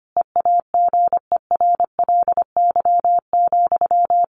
Morse code, heard as a keyed tone, about 25 wpm, 700 Hz. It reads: EAGERLY,
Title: A Sound Of Thunder 25wpm